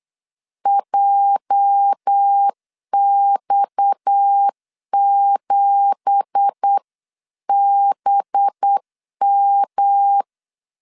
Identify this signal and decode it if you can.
Morse code, heard as a keyed tone, about 8 wpm, 800 Hz. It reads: JX7BM
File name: 8wpm